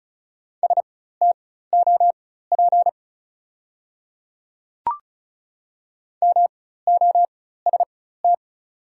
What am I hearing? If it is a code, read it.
Morse code, heard as a keyed tone, about 36 wpm, spaced as for 16 wpm, 690 Hz. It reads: STOP MOST
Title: 4x Character Spacing - Top 500 Words - 35wpm